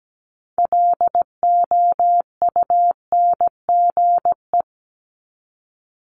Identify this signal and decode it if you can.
Morse code, heard as a keyed tone, about 17 wpm, 700 Hz. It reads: LOUNGE
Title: Multi-letter Phonemes 17wpm